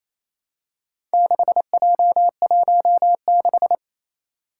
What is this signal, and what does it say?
Morse code, heard as a keyed tone, about 27 wpm, 700 Hz. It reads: 6J16
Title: Four Challenging Letter-Number Combinations 28wpm